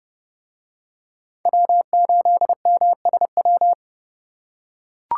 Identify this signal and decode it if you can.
Morse code, heard as a keyed tone, about 29 wpm, 700 Hz. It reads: W8MSW